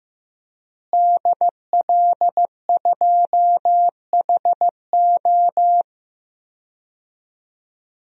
Morse code, heard as a keyed tone, about 15 wpm, 700 Hz. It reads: DL2HO
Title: DXCC-100 Call Signs - 5 Characters - Repeated 2x - 15wpm